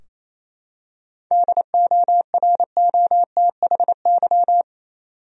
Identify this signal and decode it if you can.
Morse code, heard as a keyed tone, about 27 wpm, 700 Hz. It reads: DOROTHY
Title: USA Common Names - 28wpm